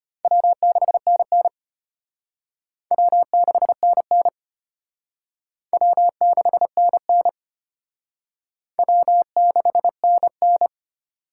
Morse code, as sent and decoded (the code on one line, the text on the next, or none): .-- -.... -. -. / .-- -.... -. -. / .-- -.... -. -. / .-- -.... -. -.
W6NN W6NN W6NN W6NN